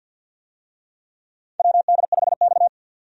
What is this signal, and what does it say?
Morse code, heard as a keyed tone, about 52 wpm, 700 Hz. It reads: WD5X
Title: USA Call Signs - 4 Characters - 50wpm